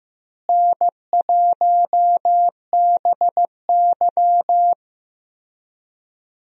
Morse code, heard as a keyed tone, about 15 wpm, 700 Hz. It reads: N1BY
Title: USA Call Signs - 4 Characters - Repeated 2x - 15wpm